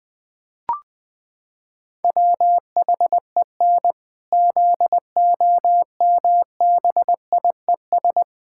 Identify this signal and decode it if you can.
Morse code, heard as a keyed tone, about 20 wpm, 700 Hz. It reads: WHEN ZOMBIES
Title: Pangram Challenge v2 - 20wpm